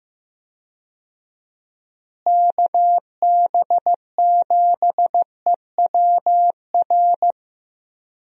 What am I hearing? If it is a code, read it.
Morse code, heard as a keyed tone, about 15 wpm, 700 Hz. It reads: KB7EWR
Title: USA Call Signs - 6 Characters - Repeated 2x - 15wpm